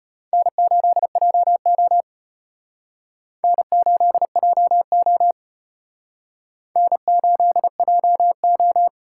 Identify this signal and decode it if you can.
Morse code, heard as a keyed tone, about 31 wpm, 700 Hz. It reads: N8JO N8JO N8JO
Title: Speed Racing - USA 1x2 Call Signs 25wpm